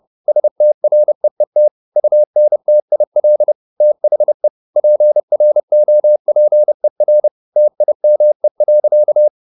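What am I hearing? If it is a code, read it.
Morse code, heard as a keyed tone, about 30 wpm, 600 Hz. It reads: STREET UNTIL THE PROPER TIME.
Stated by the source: Morse code at 29 wpm, sending Ken Follett's novel Never in English, from CW med Gnister.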